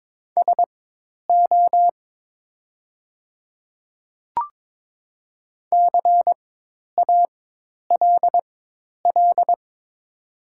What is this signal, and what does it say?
Morse code, heard as a keyed tone, about 22 wpm, spaced as for 10 wpm, 700 Hz. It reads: SO CALL
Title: 4x Character Spacing - Top 200 Words - 22wpm